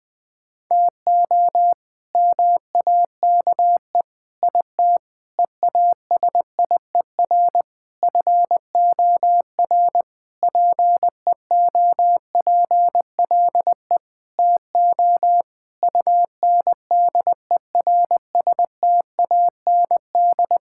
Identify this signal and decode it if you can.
Morse code, heard as a keyed tone, about 21 wpm, 700 Hz. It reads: TO MAKE IT EASIER FOR PEOPLE TO UNDERSTAND
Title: Sentences with 7-Word Phrases - Spoken Beginning - 20wpm